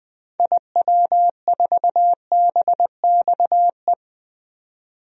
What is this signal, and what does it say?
Morse code, heard as a keyed tone, about 20 wpm, 700 Hz. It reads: IW4BXE
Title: DXCC-100 Call Signs - 6 Characters - 20wpm